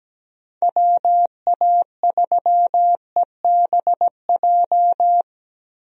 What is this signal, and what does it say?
Morse code, heard as a keyed tone, about 17 wpm, 700 Hz. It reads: WA3EBJ
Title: USA Call Signs - 6 Characters - Repeated 3x - 17wpm